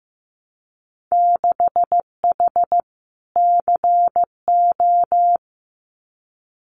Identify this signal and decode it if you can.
Morse code, heard as a keyed tone, about 15 wpm, 700 Hz. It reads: 6H CO
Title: Winter Field Day Exchanges Encore 15wpm